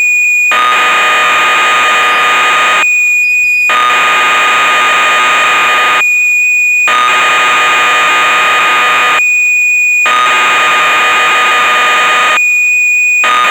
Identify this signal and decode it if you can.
Cassette data (Kansas City type).
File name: magazines / Acorn User / 1985-06 Electron Programs